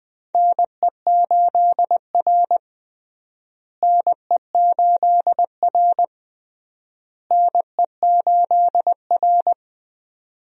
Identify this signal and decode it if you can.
Morse code, heard as a keyed tone, about 20 wpm, 700 Hz. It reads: NE8R NE8R NE8R
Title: USA Call Signs - 4 Characters - Repeated 3x - 20wpm